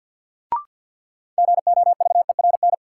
Morse code, heard as a keyed tone, about 51 wpm, 700 Hz. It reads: GOVERN
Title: Sentences from Top 1000 Words - Review Part 2 - New Word Drill - 50wpm